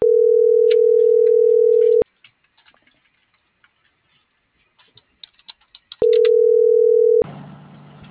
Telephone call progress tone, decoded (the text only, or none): ringback tone